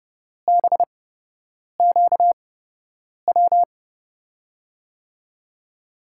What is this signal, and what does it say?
Morse code, heard as a keyed tone, about 30 wpm, spaced as for 8 wpm, 700 Hz. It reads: BQW
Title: USA Call Signs - 6 Characters - 8x Character Spacing - 30wpm